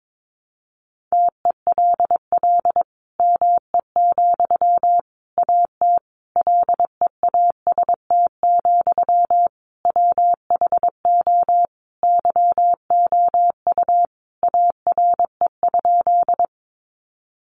Morse code, heard as a keyed tone, about 22 wpm, 700 Hz. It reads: TELL ME, AT LEAST, WHO YOU ARE?
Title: Sentences from Top 700 Words 22wpm